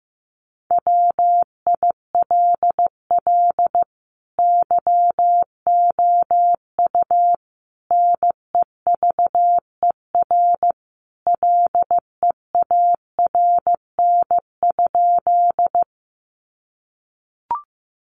Morse code, heard as a keyed tone, about 15 wpm, 700 Hz. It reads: WILL YOU NEVER LEARN?
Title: Sentences from Top 300 Words 15wpm